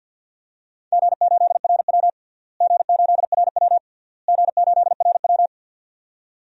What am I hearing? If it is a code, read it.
Morse code, heard as a keyed tone, about 52 wpm, 700 Hz. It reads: G8RW G8RW G8RW
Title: DXCC-100 Call Signs - 4 Characters - Repeated 3x - 50wpm